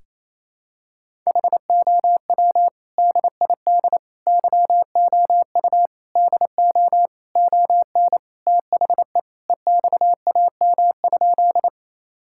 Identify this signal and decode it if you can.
Morse code, heard as a keyed tone, about 28 wpm, 700 Hz. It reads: HOW DID YOU DO ON THE EXAM?
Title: Sentences from Top 1000 Words - Review Part 5 - 28wpm